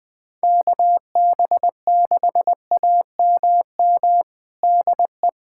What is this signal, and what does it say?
Morse code, heard as a keyed tone, about 20 wpm, 700 Hz. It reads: KB6AMM DE
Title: USA Callsign DE Callsign - 6 Characters - 20wpm